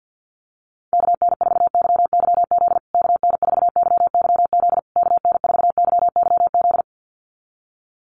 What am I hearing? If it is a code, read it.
Morse code, heard as a keyed tone, about 48 wpm, 700 Hz. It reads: KN4YYZ KN4YYZ KN4YYZ
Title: USA Call Signs Repeated 3x 50wpm